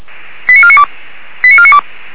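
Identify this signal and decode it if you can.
Five-tone selective call (ZVEI), heard as quick runs of five tones, tones of about 70 ms. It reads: 89492 89492
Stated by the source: ZVEI five-tone call received by SDR, hosted by feuerwehr-orlamuende (fire brigade Orlamünde).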